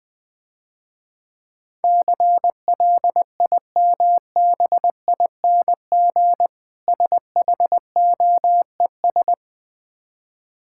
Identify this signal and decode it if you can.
Morse code, heard as a keyed tone, about 20 wpm, 690 Hz. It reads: CLIMBING SHOES
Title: Difficult Words - Speed-Racing with Hints - 20wpm